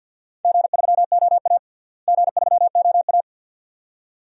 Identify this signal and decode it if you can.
Morse code, heard as a keyed tone, about 50 wpm, 700 Hz. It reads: G3OA G3OA
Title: DXCC-100 Call Signs - 4 Characters - Repeated 2x - 50wpm